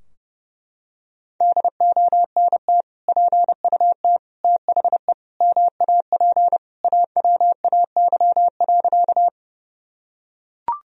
Morse code, heard as a keyed tone, about 30 wpm, 700 Hz. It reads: DONT PUT THE MAP AWAY.
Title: Sentences from Top 1000 Words - Review Part 2 - 30wpm